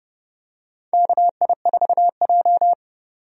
Morse code, heard as a keyed tone, about 30 wpm, 700 Hz. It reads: KI4J
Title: USA Call Signs - 4 Characters - 30wpm